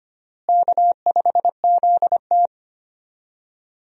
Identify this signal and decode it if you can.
Morse code, heard as a keyed tone, about 25 wpm, 700 Hz. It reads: K5ZT
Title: USA Call Signs - 4 Characters - Repeated 2x - 25wpm